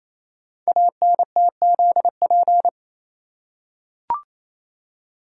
Morse code, heard as a keyed tone, about 27 wpm, 700 Hz. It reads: ANTZP
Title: Code Groups - 5 Letters - 28wpm